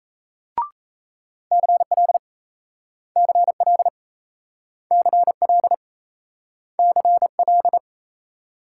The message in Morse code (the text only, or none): CL CL CL CL